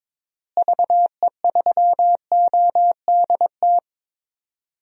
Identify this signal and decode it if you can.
Morse code, heard as a keyed tone, about 22 wpm, 700 Hz. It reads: VE3ODT
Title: DXCC-100 Call Signs - 6 Characters - Repeated 3x - 22wpm